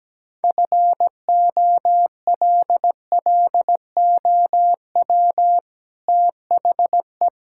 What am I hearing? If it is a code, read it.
Morse code, heard as a keyed tone, about 17 wpm, 700 Hz. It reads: FOLLOW THE
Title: Sentences from Top 1000 Words - Review Part 5 - 17wpm